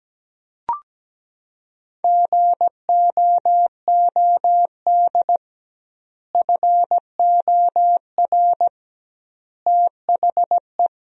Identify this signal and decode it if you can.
Morse code, heard as a keyed tone, about 17 wpm, 700 Hz. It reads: GOOD FOR THE
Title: 2x Word-Spacing - Sentences from Top 300 Words - 17wpm